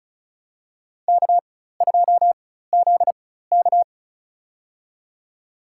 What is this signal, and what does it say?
Morse code, heard as a keyed tone, about 36 wpm, spaced as for 16 wpm, 700 Hz. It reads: K2ZK
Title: USA Call Signs - 4x Character Spacing - 35wpm